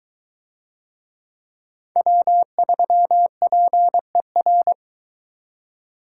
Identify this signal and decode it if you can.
Morse code, heard as a keyed tone, about 23 wpm, 700 Hz. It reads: W3PER